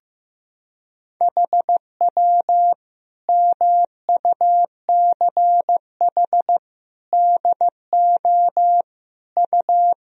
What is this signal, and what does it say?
Morse code, heard as a keyed tone, about 15 wpm, 700 Hz. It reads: HW MUCH DO U
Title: Sentences from Top 200 Words and Abbreviations 15wpm